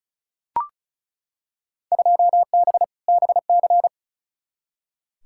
Morse code, heard as a keyed tone, about 36 wpm, 700 Hz. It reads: 2B BC
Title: Field Day Exchanges Encore - 35wpm